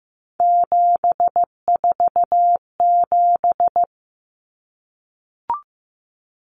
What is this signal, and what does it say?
Morse code, heard as a keyed tone, about 15 wpm, 700 Hz. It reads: 747
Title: Numbers - 3-Digit 15wpm